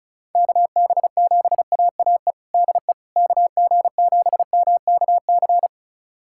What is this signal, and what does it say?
Morse code, heard as a keyed tone, about 36 wpm, 700 Hz. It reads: KB7AAE DE KG7MKC
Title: USA Callsign DE Callsign - 6 Characters - 35wpm